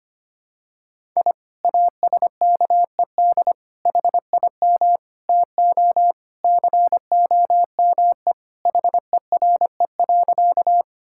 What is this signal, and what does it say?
Morse code, heard as a keyed tone, about 25 wpm, 700 Hz. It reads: I ASKED HIM TO COME HERE.
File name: practice / 25wpm / Sentences with 2-Word Phrases - 25wpm